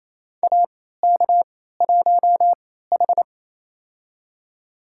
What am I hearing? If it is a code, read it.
Morse code, heard as a keyed tone, about 28 wpm, spaced as for 16 wpm, 700 Hz. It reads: AK1H
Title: USA Call Signs - 4 Characters - 3x Character Spacing - 28wpm